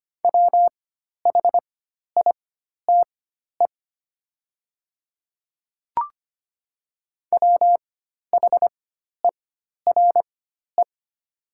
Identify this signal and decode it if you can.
Morse code, heard as a keyed tone, about 25 wpm, spaced as for 12 wpm, 700 Hz. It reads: WHITE WHERE
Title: 4x Character Spacing - Top 500 Words - 25wpm